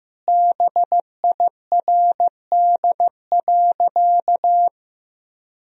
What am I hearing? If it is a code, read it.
Morse code, heard as a keyed tone, about 15 wpm, 700 Hz. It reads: BIRD.
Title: Sentences from Top 1000 Words - Review Part 2 - 15wpm